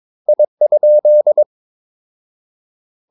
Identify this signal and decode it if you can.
Morse code, heard as a keyed tone, about 22 wpm, 600 Hz. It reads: I?
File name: Ferie QRS 22wpm